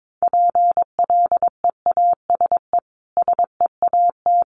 Morse code, heard as a keyed tone, about 22 wpm, 700 Hz. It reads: PLEASE SEAT
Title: Sentences from Top 1500 Words 22wpm